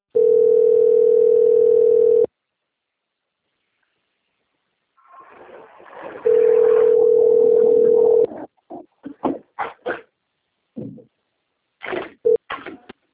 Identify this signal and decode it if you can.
Ringback tone (North America), 440 + 480 Hz.
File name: out- - - - - .4902